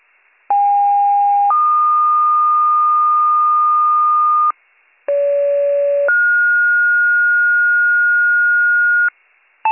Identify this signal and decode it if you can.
Two-tone page, heard as a one-second tone, then a three-second tone, about 810 then 1220 Hz, about 570 then 1430 Hz, with noise. It is heard from 0.5 to 9.1 s.